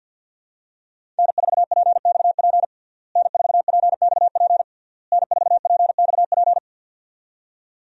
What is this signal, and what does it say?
Morse code, heard as a keyed tone, about 46 wpm, 700 Hz. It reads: N4PXP N4PXP N4PXP
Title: USA Call Signs - 5 Characters - Repeated 3x - 50wpm